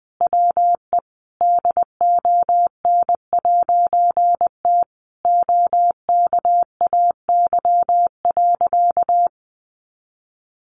Morse code, heard as a keyed tone, about 20 wpm, 700 Hz. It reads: WE DON'T OKAY.
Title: A Sound Of Thunder 20wpm